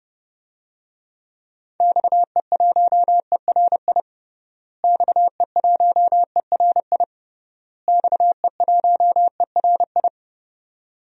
Morse code, heard as a keyed tone, about 32 wpm, 700 Hz. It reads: XE1ERI XE1ERI XE1ERI
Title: DXCC-100 Call Signs - 6 Characters - Repeated 3x - 30wpm